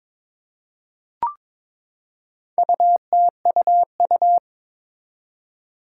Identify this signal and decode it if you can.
Morse code, heard as a keyed tone, about 22 wpm, 700 Hz. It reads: UTUU
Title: Four Challenging Letter-Number Combinations 22wpm